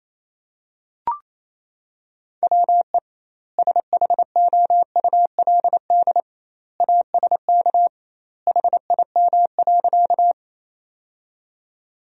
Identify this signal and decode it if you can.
Morse code, heard as a keyed tone, about 27 wpm, 700 Hz. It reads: WE SHOULD ASK HIM.
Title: 2x Word-Spacing - Sentences from Top 300 Words - 28wpm